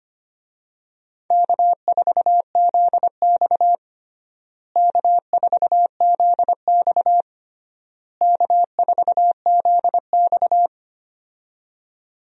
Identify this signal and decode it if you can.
Morse code, heard as a keyed tone, about 25 wpm, 700 Hz. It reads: K4ZX K4ZX K4ZX
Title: USA Call Signs - 4 Characters - Repeated 3x - 25wpm